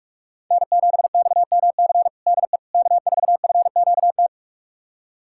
Morse code, heard as a keyed tone, about 41 wpm, 690 Hz. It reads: N7XMC DE K4FQT
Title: USA Callsign DE Callsign 45wpm